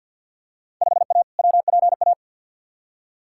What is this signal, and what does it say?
Morse code, heard as a keyed tone, about 46 wpm, 700 Hz. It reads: 5A WPA
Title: Field Day Exchanges Encore - 50wpm